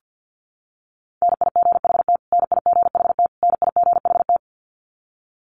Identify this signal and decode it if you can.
Morse code, heard as a keyed tone, about 50 wpm, 700 Hz. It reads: NIGHT NIGHT NIGHT
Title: Top 300 Words Repeated 50wpm